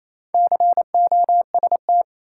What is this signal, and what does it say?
Morse code, heard as a keyed tone, about 28 wpm, 700 Hz. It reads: COST
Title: Sets of 3 Words - New Word Drill - 28wpm